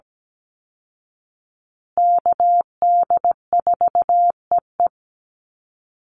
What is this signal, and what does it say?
Morse code, heard as a keyed tone, about 17 wpm, 700 Hz. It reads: KD4EE
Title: Call Signs Repeated 3x - 17wpm